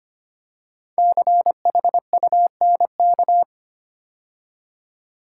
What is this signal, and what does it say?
Morse code, heard as a keyed tone, about 26 wpm, 700 Hz. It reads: CHUNK
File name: 5 Letter Words - First 3 Letters Given - 25wpm